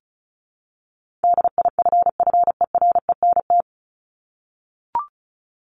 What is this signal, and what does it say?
Morse code, heard as a keyed tone, about 34 wpm, 700 Hz. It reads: DIFFERENT
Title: Top 501-750 Words 35wpm